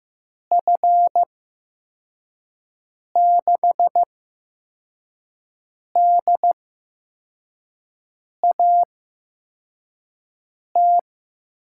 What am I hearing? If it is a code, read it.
Morse code, heard as a keyed tone, about 15 wpm, spaced as for 4 wpm, 700 Hz. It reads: F6DAT
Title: USA Call Signs - 6 Characters - 8x Character Spacing - 15wpm